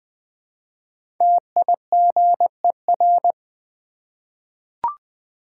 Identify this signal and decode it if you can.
Morse code, heard as a keyed tone, about 20 wpm, 700 Hz. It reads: TIGER
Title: Easy Words with Hints - 20wpm